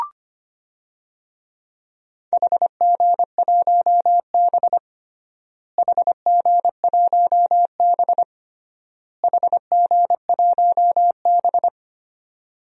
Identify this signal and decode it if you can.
Morse code, heard as a keyed tone, about 25 wpm, 700 Hz. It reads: HG1B HG1B HG1B
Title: DXCC-100 Call Signs - 4 Characters - Repeated 3x - 25wpm